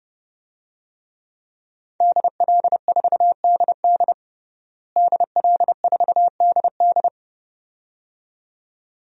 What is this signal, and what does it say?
Morse code, heard as a keyed tone, about 31 wpm, 700 Hz. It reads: DL4DD DL4DD
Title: DXCC-100 Call Signs - 5 Characters - Repeated 2x - 30wpm